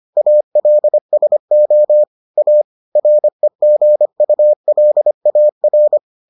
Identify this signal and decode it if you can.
Morse code, heard as a keyed tone, about 25 wpm, 600 Hz. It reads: ALSO A REGULAR